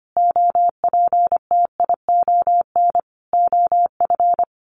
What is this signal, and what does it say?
Morse code, heard as a keyed tone, about 25 wpm, 700 Hz. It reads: OPTION OF
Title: Sentences from Top 1300 Words 25wpm